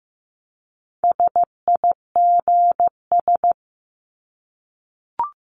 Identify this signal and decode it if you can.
Morse code, heard as a keyed tone, about 15 wpm, 700 Hz. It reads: SIGS
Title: Abbreviations Repeated 3x - 15wpm